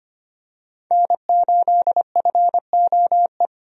Morse code, heard as a keyed tone, about 25 wpm, 700 Hz. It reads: N8FOE
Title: USA Call Signs - 5 Characters - Repeated 2x - 25wpm